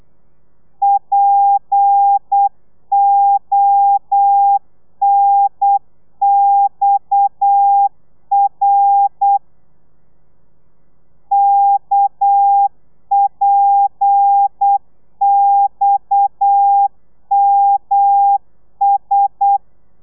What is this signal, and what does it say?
Morse code, heard as a keyed tone, about 8 wpm, 800 Hz. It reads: PONXR KPXMS